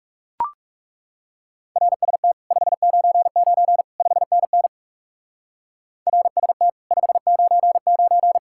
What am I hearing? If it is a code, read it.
Morse code, heard as a keyed tone, about 36 wpm, 700 Hz. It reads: RST 599 5NN RST 599
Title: Speed-Racing Common QSO Phrases - 30wpm